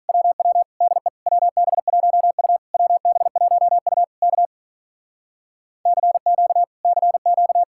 Morse code, heard as a keyed tone, about 43 wpm, 700 Hz. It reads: WW DE WB1U WB1U K CQ CQ
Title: Speed-Racing Common QSO Phrases - 35wpm